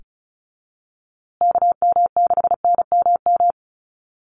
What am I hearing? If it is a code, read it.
Morse code, heard as a keyed tone, about 34 wpm, 700 Hz. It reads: KM6NMM
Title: USA Call Signs Repeated 2x 35wpm